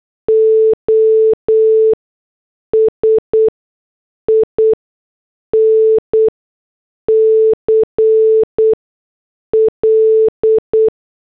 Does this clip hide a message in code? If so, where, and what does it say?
Yes: Morse code from 0.3 to 10.9 s, heard as a keyed tone, about 8 wpm, spaced as for 6 wpm, 440 Hz. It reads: OSINCL